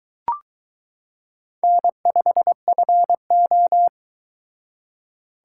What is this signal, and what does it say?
Morse code, heard as a keyed tone, about 23 wpm, 700 Hz. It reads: N5FO